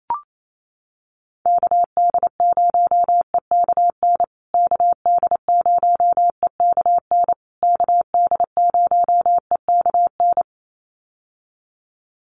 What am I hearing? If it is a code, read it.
Morse code, heard as a keyed tone, about 28 wpm, 700 Hz. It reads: KD0EKN KD0EKN KD0EKN
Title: USA Call Signs Repeated 3x 28wpm